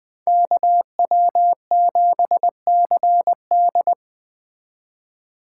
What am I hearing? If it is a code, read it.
Morse code, heard as a keyed tone, about 20 wpm, 700 Hz. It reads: KW7CD
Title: Speed Racing - USA 2x2 Call Signs 17wpm